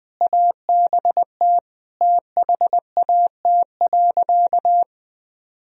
Morse code, heard as a keyed tone, about 21 wpm, 700 Hz. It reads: ABT THAT.
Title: Sentences from Top 200 Words and Abbreviations 20wpm